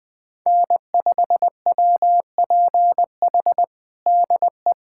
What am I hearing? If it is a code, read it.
Morse code, heard as a keyed tone, about 21 wpm, 700 Hz. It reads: N5WPH DE
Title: USA Callsign DE Callsign 20wpm